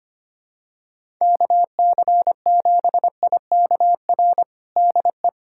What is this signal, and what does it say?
Morse code, heard as a keyed tone, about 25 wpm, 700 Hz. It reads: KC7IKR DE
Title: USA Callsign DE Callsign 25wpm